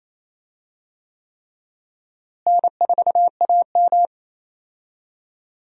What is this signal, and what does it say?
Morse code, heard as a keyed tone, about 28 wpm, 700 Hz. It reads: N4AM
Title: USA Call Signs - 4 Characters - Repeated 3x - 28wpm